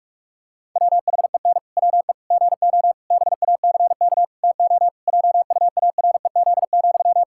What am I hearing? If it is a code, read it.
Morse code, heard as a keyed tone, about 46 wpm, 700 Hz. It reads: WHEN WE GO BACK TO JUAREZ,
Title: Pangram Challenge v2 - 45wpm